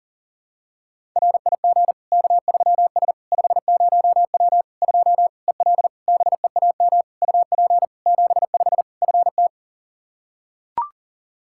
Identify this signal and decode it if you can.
Morse code, heard as a keyed tone, about 40 wpm, 700 Hz. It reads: RIG K3S 50W 2 EL BEAM UP 75 FT
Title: Practice QSO - Exchange 2 - Info - 40wpm